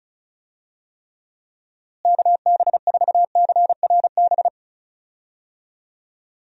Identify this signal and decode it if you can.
Morse code, heard as a keyed tone, about 36 wpm, 700 Hz. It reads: KB4CRB
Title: USA Call Signs - 6 Characters - Repeated 2x - 35wpm